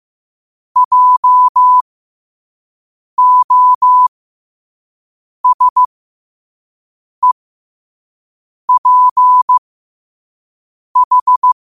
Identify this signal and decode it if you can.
Morse code, heard as a keyed tone, about 15 wpm, spaced as for 5 wpm, 1000 Hz. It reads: JOSEPH
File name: qotdfiles / 5wpm